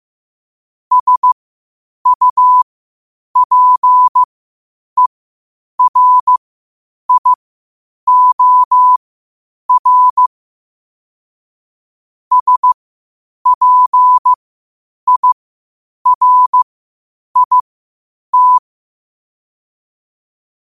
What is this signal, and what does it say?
Morse code, heard as a keyed tone, about 15 wpm, spaced as for 8 wpm, 1000 Hz. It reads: SUPERIOR SPIRIT
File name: qotdfiles / 8wpm